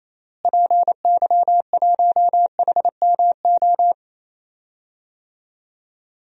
Morse code, heard as a keyed tone, about 28 wpm, 700 Hz. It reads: PY1HMO
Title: DXCC-100 Call Signs - 6 Characters - Repeated 2x - 28wpm